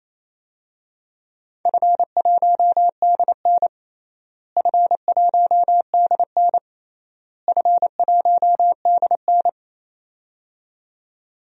Morse code, heard as a keyed tone, about 28 wpm, 700 Hz. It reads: F1DN F1DN F1DN
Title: DXCC-100 Call Signs - 4 Characters - Repeated 3x - 28wpm